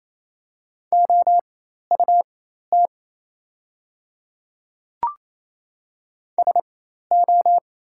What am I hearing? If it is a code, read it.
Morse code, heard as a keyed tone, about 28 wpm, spaced as for 13 wpm, 700 Hz. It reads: OUT SO
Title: 4x Character Spacing - Top 100 Words - 28wpm